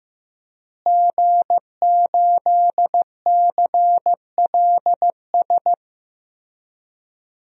Morse code, heard as a keyed tone, about 15 wpm, 700 Hz. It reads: G8CLS
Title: DXCC-100 Call Signs - 5 Characters - Repeated 3x - 15wpm